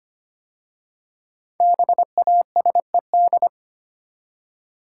Morse code, heard as a keyed tone, about 25 wpm, 700 Hz. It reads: BASED
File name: 5 Letter Words - First 3 Letters Given - 25wpm